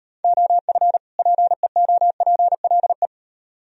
Morse code, heard as a keyed tone, about 38 wpm, 700 Hz. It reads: OF PEOPLE